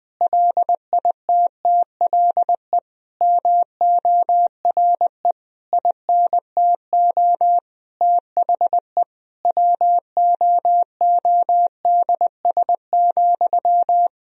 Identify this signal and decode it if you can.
Morse code, heard as a keyed tone, about 20 wpm, 700 Hz. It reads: LITTLE MORE INTO THE WOODS,
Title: Sentences from Top 1000 Words - Review Part 2 - 20wpm